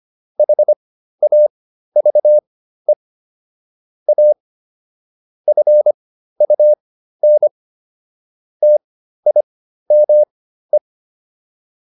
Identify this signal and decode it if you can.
Morse code, heard as a keyed tone, about 25 wpm, spaced as for 13 wpm, 600 Hz. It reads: HAVE A FUN TIME